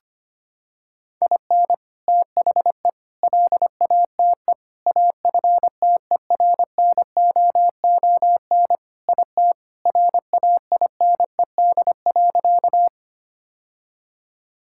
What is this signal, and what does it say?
Morse code, heard as a keyed tone, about 25 wpm, 700 Hz. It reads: IN THE LATE AFTERNOON IT RAINED.
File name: practice / 25wpm / Sentences with 3-Word Phrases - 25wpm